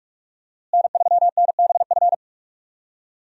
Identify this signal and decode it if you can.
Morse code, heard as a keyed tone, about 47 wpm, 700 Hz. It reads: N3NBF
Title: USA Call Signs - 5 Characters - Repeated 2x - 45wpm